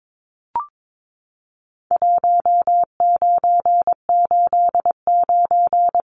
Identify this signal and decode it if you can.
Morse code, heard as a keyed tone, about 22 wpm, 700 Hz. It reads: 1989
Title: Numbers - 4-Digit 22wpm